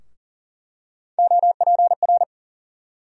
Morse code, heard as a keyed tone, about 40 wpm, 700 Hz. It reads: OPR
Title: USA Call Sign Postfixes - 3 Letters 40wpm